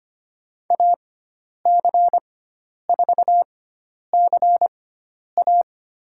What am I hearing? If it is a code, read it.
Morse code, heard as a keyed tone, about 25 wpm, spaced as for 10 wpm, 700 Hz. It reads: AC4CA
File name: Lesson 16g - Letter c - Call Signs 25wpm